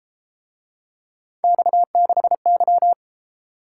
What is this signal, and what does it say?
Morse code, heard as a keyed tone, about 30 wpm, 700 Hz. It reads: X6Y